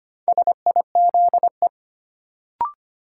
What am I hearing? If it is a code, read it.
Morse code, heard as a keyed tone, about 25 wpm, 700 Hz. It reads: SIZE
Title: Sentences from Top 400 Words - New Word Drill - 25wpm